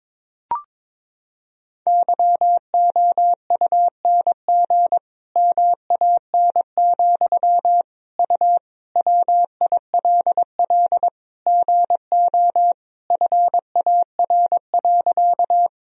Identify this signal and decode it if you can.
Morse code, heard as a keyed tone, about 22 wpm, 700 Hz. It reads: YOUNG MAN, U WILL GO FAR.
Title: Sentences from Top 400 Words and Abbreviations 22wpm